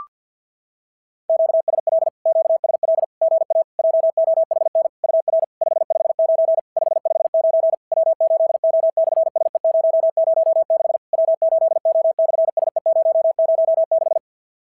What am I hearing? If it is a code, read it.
Morse code, heard as a keyed tone, about 54 wpm, 650 Hz. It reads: QSL QSL GA JOHN UR 559 559 W8O/SE006 W8O/SE006
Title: SOTA QSO Practice 50wpm